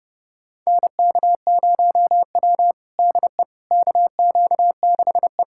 Morse code, heard as a keyed tone, about 30 wpm, 700 Hz. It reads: NK0W DE KQ6E
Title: USA Callsign DE Callsign - 4 Characters - 30wpm